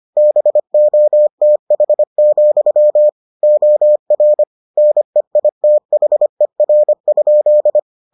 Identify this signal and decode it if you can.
Morse code, heard as a keyed tone, about 25 wpm, 600 Hz. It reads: BOTH, OR NEITHER?